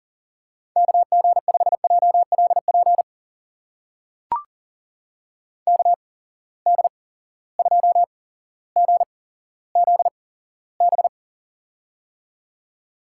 Morse code, heard as a keyed tone, about 40 wpm, spaced as for 17 wpm, 700 Hz. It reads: KG5JLP KD2GZB
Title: USA Call Signs - 6 Characters - 8x Character Spacing - 40wpm